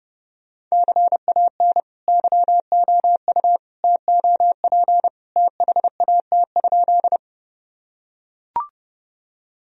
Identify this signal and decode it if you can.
Morse code, heard as a keyed tone, about 30 wpm, 700 Hz. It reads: CAN YOU TOP THAT?
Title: Sentences from Top 1000 Words - Review Part 2 - 30wpm